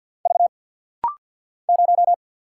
Morse code, heard as a keyed tone, about 48 wpm, 700 Hz. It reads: V 0
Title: Single Letter-Number - Rapid-Fire 50wpm